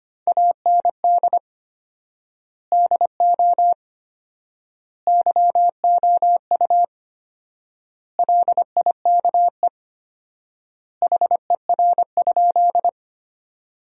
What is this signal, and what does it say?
Morse code, heard as a keyed tone, about 25 wpm, 700 Hz. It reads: AND DO YOU LIKE HER?
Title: 4x Word-Spacing - Sentences from Top 100 Words - 25wpm